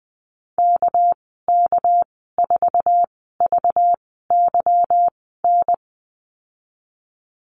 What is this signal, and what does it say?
Morse code, heard as a keyed tone, about 20 wpm, spaced as for 14 wpm, 700 Hz. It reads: KK4VYN
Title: 219 Lesson 35g - Number 0 - Call Signs 20wpm